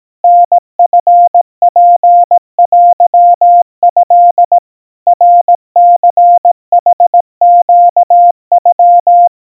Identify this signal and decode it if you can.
Morse code (Japanese Wabun code), heard as a keyed tone, about 18 wpm, 700 Hz. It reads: タチツテト ナニヌネノ
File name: 60 cpm 700 Hz Crispy